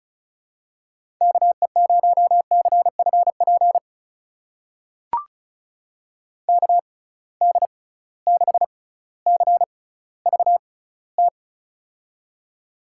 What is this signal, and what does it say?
Morse code, heard as a keyed tone, about 34 wpm, spaced as for 18 wpm, 700 Hz. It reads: KE0CFP KD6CVT
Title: USA Call Signs - 6 Characters - 6x Character Spacing - 35wpm